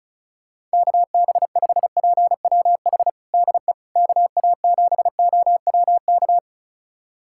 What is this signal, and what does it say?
Morse code, heard as a keyed tone, about 36 wpm, 700 Hz. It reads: KB5PWH DE KA7OWK
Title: USA Callsign DE Callsign 35wpm